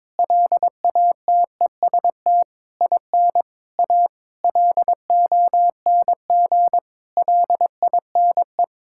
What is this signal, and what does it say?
Morse code, heard as a keyed tone, about 22 wpm, 700 Hz. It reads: LATEST IN A LONG LINE
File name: practice / 22wpm / Sentences with 8-Word Phrases - 22wpm